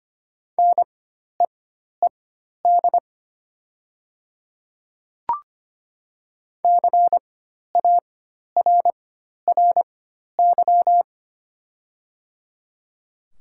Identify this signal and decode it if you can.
Morse code, heard as a keyed tone, about 25 wpm, spaced as for 12 wpm, 700 Hz. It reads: NEED CARRY
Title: 4x Character Spacing - Top 500 Words - 25wpm